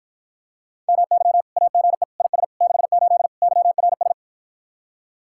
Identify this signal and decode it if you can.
Morse code, heard as a keyed tone, about 59 wpm, 700 Hz. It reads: MY AGE IS 68 YRS